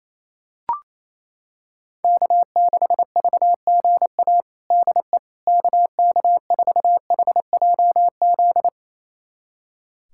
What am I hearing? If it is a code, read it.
Morse code, heard as a keyed tone, about 28 wpm, 700 Hz. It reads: K6VGA DE KK4HJZ
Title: USA Callsign DE Callsign 28wpm